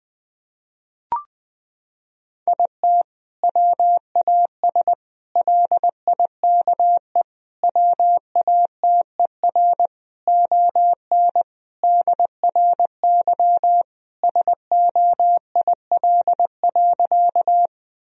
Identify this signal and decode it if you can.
Morse code, heard as a keyed tone, about 20 wpm, 700 Hz. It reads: IT WAS LIKE WATER ON DRY SOIL.
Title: Sentences from Top 1000 Words - Review Part 4 - 20wpm